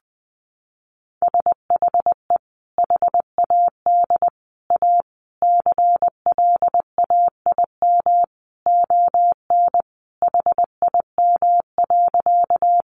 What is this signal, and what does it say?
Morse code, heard as a keyed tone, about 20 wpm, 700 Hz. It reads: SHE HAD A CLAIM ON HIM.
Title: Sentences from Top 1500 Words 20wpm